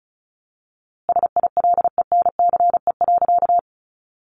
Morse code, heard as a keyed tone, about 34 wpm, 700 Hz. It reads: SILENCE.
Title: A Sound Of Thunder 35wpm